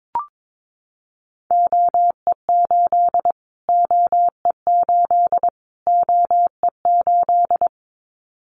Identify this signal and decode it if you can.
Morse code, heard as a keyed tone, about 22 wpm, 700 Hz. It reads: OE8 OE8 OE8
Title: Callsign Prefixes for Top 25 DXCC - Repeated 3x - 22wpm